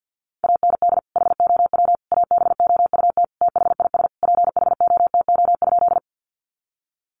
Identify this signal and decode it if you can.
Morse code, heard as a keyed tone, about 49 wpm, 700 Hz. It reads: AND HOW ABOUT THIS PHOTO?
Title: Sentences from Top 1200 Words 50wpm